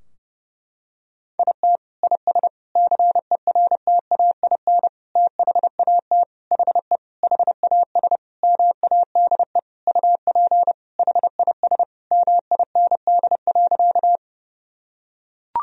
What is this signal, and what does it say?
Morse code, heard as a keyed tone, about 30 wpm, 700 Hz. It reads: IT IS CERTAIN THAT HE HAS MADE UP HIS MIND.
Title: Sentences from Top 1000 Words - Review Part 2 - 30wpm